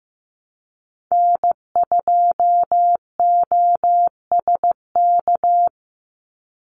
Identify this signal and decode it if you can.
Morse code, heard as a keyed tone, about 15 wpm, 700 Hz. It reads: N2OSK
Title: USA Call Signs 15wpm